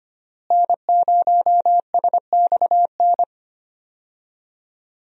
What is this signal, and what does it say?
Morse code, heard as a keyed tone, about 25 wpm, 700 Hz. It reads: N0SXN
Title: USA Call Signs - 5 Characters - Repeated 2x - 25wpm